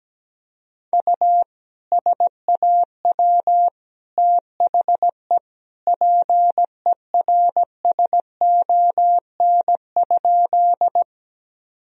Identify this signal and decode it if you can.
Morse code, heard as a keyed tone, about 17 wpm, 700 Hz. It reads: U SAW THE PERSON?
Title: Sentences from Top 400 Words and Abbreviations 17wpm